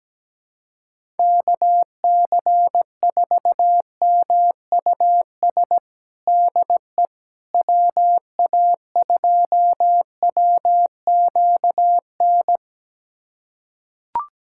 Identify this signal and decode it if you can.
Morse code, heard as a keyed tone, about 17 wpm, 700 Hz. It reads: KC4MUS DE WA2WQN
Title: USA Callsign DE Callsign 17wpm